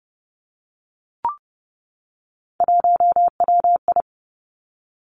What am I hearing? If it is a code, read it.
Morse code, heard as a keyed tone, about 30 wpm, 700 Hz. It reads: 1WI
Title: Three Challenging Letter-Number Combinations 30wpm